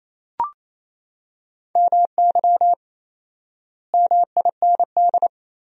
Morse code, heard as a keyed tone, about 28 wpm, 700 Hz. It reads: MY MIND
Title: 4x Word-Spacing - Sentences from Top 500 Words - 28wpm